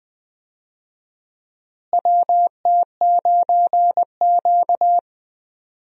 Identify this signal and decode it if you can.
Morse code, heard as a keyed tone, about 20 wpm, 700 Hz. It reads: WT9Q
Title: USA Call Signs - 4 Characters - Repeated 3x - 20wpm